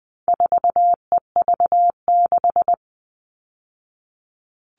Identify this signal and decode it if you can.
Morse code, heard as a keyed tone, about 20 wpm, 700 Hz. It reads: 4EV6
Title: All Cut Numbers - 4-Digit 20wpm